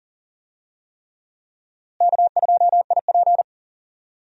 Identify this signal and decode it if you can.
Morse code, heard as a keyed tone, about 40 wpm, 700 Hz. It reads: K2IP
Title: USA Call Signs - 4 Characters - Repeated 3x - 40wpm